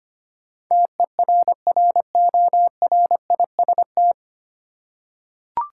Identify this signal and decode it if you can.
Morse code, heard as a keyed tone, about 25 wpm, 700 Hz. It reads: TERRORIST
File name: Nine Letter Words - 25wpm